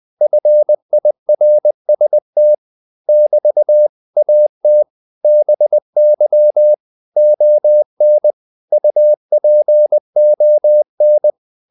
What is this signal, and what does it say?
Morse code, heard as a keyed tone, about 20 wpm, 600 Hz. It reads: FIRST <BT> AT BY ON UPON
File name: CW / Practice / 500MostCommonEnglishWords20